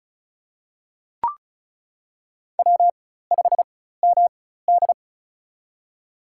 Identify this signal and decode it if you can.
Morse code, heard as a keyed tone, about 36 wpm, spaced as for 16 wpm, 700 Hz. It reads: W5MD
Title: USA Call Signs - 4x Character Spacing - 35wpm